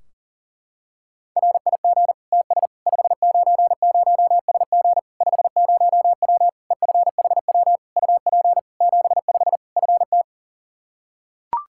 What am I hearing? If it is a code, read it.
Morse code, heard as a keyed tone, about 40 wpm, 700 Hz. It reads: RIG TS 590SG 50W EFHW UP 75 FT
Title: Practice QSO - Exchange 2 - Info - 40wpm